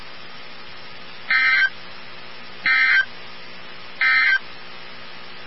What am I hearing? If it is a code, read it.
Emergency Alert System (SAME) tones, heard as harsh buzzing bursts. It is heard from 1.2 to 4.5 s.
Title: NOAA Weather Radio 162.550